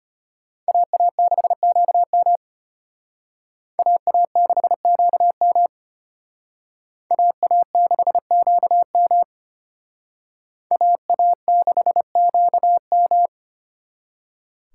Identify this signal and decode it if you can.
Morse code, heard as a keyed tone, about 32 wpm, 700 Hz. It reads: AA6QM AA6QM AA6QM AA6QM